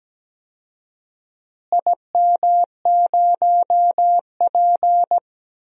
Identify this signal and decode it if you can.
Morse code, heard as a keyed tone, about 17 wpm, 700 Hz. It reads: IM0P
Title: DXCC-100 Call Signs - 4 Characters - Repeated 3x - 17wpm